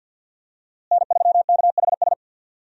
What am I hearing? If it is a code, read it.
Morse code, heard as a keyed tone, about 50 wpm, 700 Hz. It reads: N3KHS
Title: USA Call Signs - 5 Characters - Repeated 3x - 50wpm